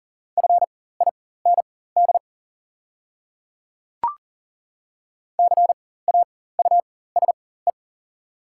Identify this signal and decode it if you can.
Morse code, heard as a keyed tone, about 39 wpm, spaced as for 19 wpm, 700 Hz. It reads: FIND CAUSE
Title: 4x Character Spacing - Top 200 Words - 40wpm